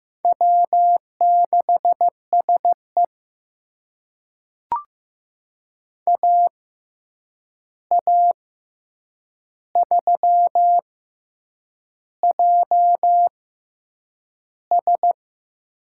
Morse code, heard as a keyed tone, about 15 wpm, spaced as for 7 wpm, 700 Hz. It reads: W6SE AA3JS
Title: USA Call Signs - 6x Character Spacing - 15wpm